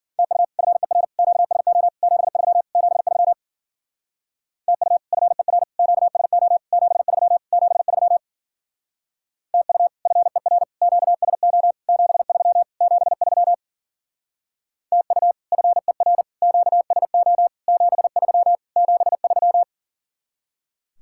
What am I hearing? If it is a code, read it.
Morse code, heard as a keyed tone, about 50 wpm, 700 Hz. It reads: TU FER QSO 73 73 TU FER QSO 73 73 TU FER QSO 73 73 TU FER QSO 73 73
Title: Speed-Racing Common QSO Phrases - 40wpm